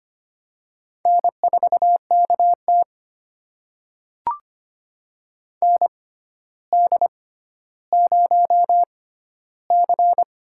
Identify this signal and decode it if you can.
Morse code, heard as a keyed tone, about 25 wpm, spaced as for 13 wpm, 700 Hz. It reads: N4KT ND0C
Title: USA Call Signs - 4 Characters - 6x Character Spacing - 25wpm